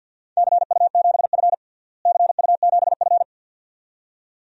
Morse code, heard as a keyed tone, about 52 wpm, 700 Hz. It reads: CU7F CU7F
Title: DXCC-100 Call Signs - 4 Characters - Repeated 2x - 50wpm